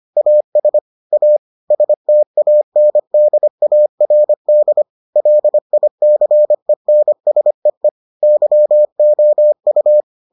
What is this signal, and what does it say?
Morse code, heard as a keyed tone, about 25 wpm, 600 Hz. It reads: AS A STANDARD LICENSEE YOU